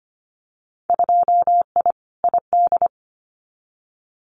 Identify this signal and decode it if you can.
Morse code, heard as a keyed tone, about 25 wpm, 700 Hz. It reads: 2I ID
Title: Winter Field Day Exchanges Encore 25wpm